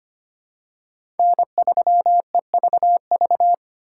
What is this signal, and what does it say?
Morse code, heard as a keyed tone, about 25 wpm, 700 Hz. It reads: N3EVV
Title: USA Call Signs - 5 Characters - 3x Character Spacing - 25wpm